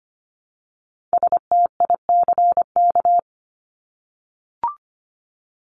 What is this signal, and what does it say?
Morse code, heard as a keyed tone, about 25 wpm, 700 Hz. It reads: STICK
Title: Top 1001-1250 Words 25wpm